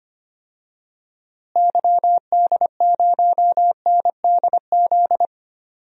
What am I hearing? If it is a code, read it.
Morse code, heard as a keyed tone, about 25 wpm, 700 Hz. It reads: YD0NDZ
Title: DXCC-100 Call Signs - 6 Characters - Repeated 3x - 25wpm